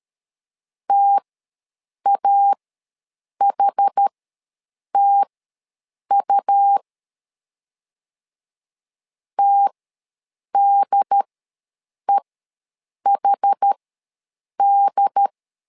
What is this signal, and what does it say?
Morse code, heard as a keyed tone, about 13 wpm, spaced as for 7 wpm, 800 Hz. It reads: TAHTU TDEHD
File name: LEZIONI / 2 UHDN / 2 UHD ws 12wpm 1722